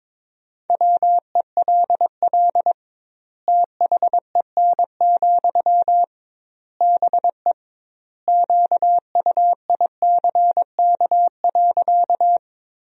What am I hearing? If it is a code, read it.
Morse code, heard as a keyed tone, about 22 wpm, 700 Hz. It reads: WELL THEN, BE QUICK.
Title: 2x Word-Spacing - Sentences from Top 500 Words - 22wpm